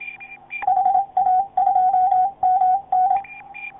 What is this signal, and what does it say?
Morse code, heard as a keyed tone, about 26 wpm, 740 Hz, with radio noise. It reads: HA2MN